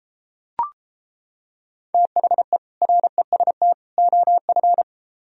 Morse code, heard as a keyed tone, about 33 wpm, 700 Hz. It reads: THE REST OF